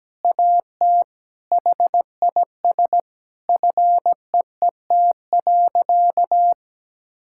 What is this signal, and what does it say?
Morse code, heard as a keyed tone, about 17 wpm, 700 Hz. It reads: AT HIS FEET.
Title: Sentences from Top 1000 Words - Review Part 6 - 17wpm